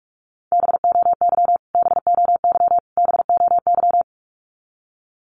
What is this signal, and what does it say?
Morse code, heard as a keyed tone, about 44 wpm, 700 Hz. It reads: BOY BOY BOY
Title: Top 500 Words Repeated 45wpm